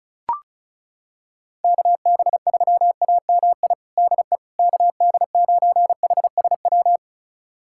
Morse code, heard as a keyed tone, about 36 wpm, 700 Hz. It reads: KB3AMI DE KD9HSW
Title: USA Callsign DE Callsign - 6 Characters - 35wpm